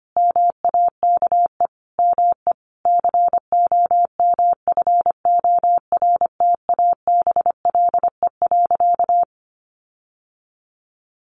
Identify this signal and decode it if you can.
Morse code, heard as a keyed tone, about 25 wpm, 700 Hz. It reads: MAKE ME COMFORTABLE.
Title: Sentences from Top 1500 Words 25wpm